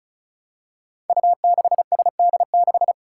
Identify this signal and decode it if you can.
Morse code, heard as a keyed tone, about 36 wpm, 700 Hz. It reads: U6SD6
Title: Five Challenging Letter-Number Combinations 35wpm